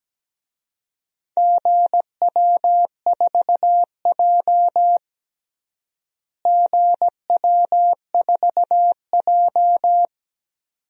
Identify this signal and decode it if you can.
Morse code, heard as a keyed tone, about 17 wpm, 700 Hz. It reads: GW4J GW4J
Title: DXCC-100 Call Signs - 4 Characters - Repeated 2x - 17wpm